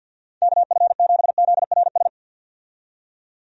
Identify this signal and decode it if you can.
Morse code, heard as a keyed tone, about 50 wpm, 700 Hz. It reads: KF7ZRS
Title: USA Call Signs - 6 Characters - 50wpm